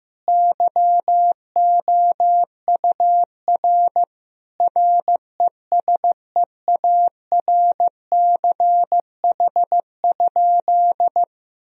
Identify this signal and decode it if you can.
Morse code, heard as a keyed tone, about 15 wpm, 700 Hz. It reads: YOUR RESEARCH?
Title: Sentences from Top 1000 Words - Review Part 4 - 15wpm